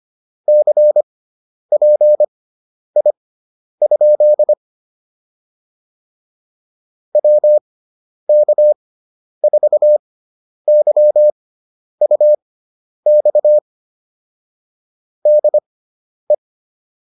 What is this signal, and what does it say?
Morse code, heard as a keyed tone, about 25 wpm, spaced as for 10 wpm, 600 Hz. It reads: CPI? WK4YUX DE